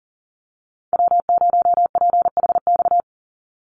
Morse code, heard as a keyed tone, about 40 wpm, 700 Hz. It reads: W0PHX
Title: Call Signs Repeated 3x - 40wpm